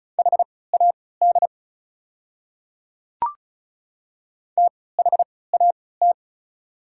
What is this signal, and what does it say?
Morse code, heard as a keyed tone, about 36 wpm, spaced as for 20 wpm, 700 Hz. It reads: HAD THAT
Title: 3x Character Spacing - Top 200 Words - 35wpm